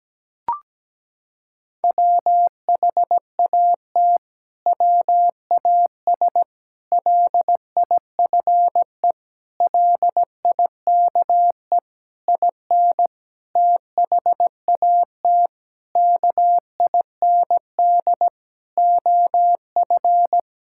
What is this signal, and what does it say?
Morse code, heard as a keyed tone, about 17 wpm, 700 Hz. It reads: WHAT WAS LIFE LIKE IN THAT KIND OF
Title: Sentences from Top 1000 Words - Review Part 5 - 17wpm